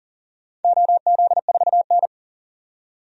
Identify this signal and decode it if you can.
Morse code, heard as a keyed tone, about 39 wpm, 700 Hz. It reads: OZ4N
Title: DXCC-100 Call Signs - 4 Characters - 40wpm